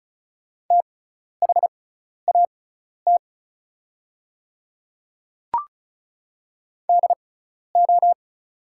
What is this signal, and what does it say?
Morse code, heard as a keyed tone, about 36 wpm, spaced as for 12 wpm, 700 Hz. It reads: THAT DO
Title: 6x Character Spacing - Top 100 Words - 35wpm